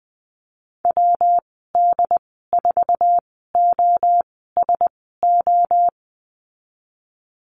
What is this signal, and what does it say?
Morse code, heard as a keyed tone, about 20 wpm, spaced as for 15 wpm, 700 Hz. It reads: WD4OSO
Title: 122 Lesson 20g - Number 3 - Call Signs 20wpm